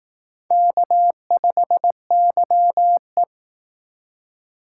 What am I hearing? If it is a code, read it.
Morse code, heard as a keyed tone, about 19 wpm, 700 Hz. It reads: K5YE
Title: Speed Racing - USA Call Signs 15wpm